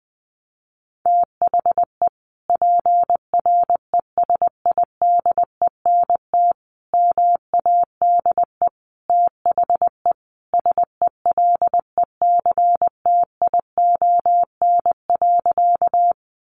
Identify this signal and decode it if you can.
Morse code, heard as a keyed tone, about 20 wpm, 700 Hz. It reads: THE PRESIDENT MADE THE SELECTION.
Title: Sentences from Top 1400 Words 20wpm